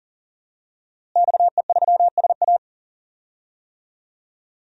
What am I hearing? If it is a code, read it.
Morse code, heard as a keyed tone, about 40 wpm, 700 Hz. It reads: XE3SA
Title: DXCC-100 Call Signs - 5 Characters - Repeated 2x - 40wpm